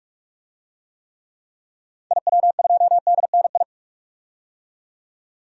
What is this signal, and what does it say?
Morse code, heard as a keyed tone, about 46 wpm, 700 Hz. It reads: IW2DNI
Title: DXCC-100 Call Signs - 6 Characters - 45wpm